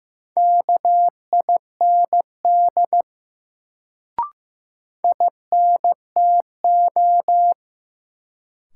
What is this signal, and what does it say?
Morse code, heard as a keyed tone, about 15 wpm, 700 Hz. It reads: KIND INTO
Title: Sets of 2 Words - New Word Drill - 15wpm